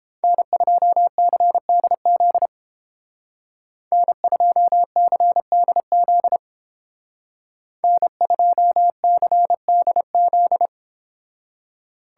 Morse code, heard as a keyed tone, about 31 wpm, 700 Hz. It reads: N2CDZ N2CDZ N2CDZ